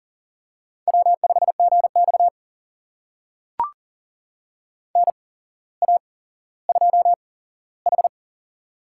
Morse code, heard as a keyed tone, about 40 wpm, spaced as for 17 wpm, 700 Hz. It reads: W5GX NA2H
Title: USA Call Signs - 4 Characters - 8x Character Spacing - 40wpm